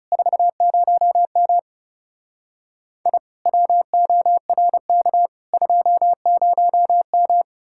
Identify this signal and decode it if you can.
Morse code, heard as a keyed tone, about 37 wpm, 700 Hz. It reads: 40M I WORK 20M
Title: Speed-Racing Common QSO Phrases Encore - 30wpm